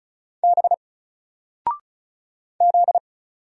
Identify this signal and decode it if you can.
Morse code, heard as a keyed tone, about 36 wpm, 700 Hz. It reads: B Z
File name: Single-Letter-Number-35wpm